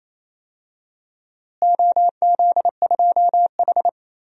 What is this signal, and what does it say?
Morse code, heard as a keyed tone, about 28 wpm, 700 Hz. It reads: OZ2H
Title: DXCC-100 Call Signs - 4 Characters - Repeated 3x - 28wpm